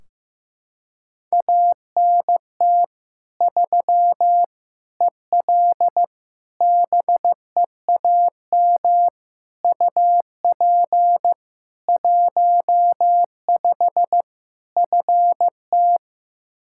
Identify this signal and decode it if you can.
Morse code, heard as a keyed tone, about 14 wpm, 700 Hz. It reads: ANT 3 EL BEAM UP 15 FT
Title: K1USN SST QSO Practice 15wpm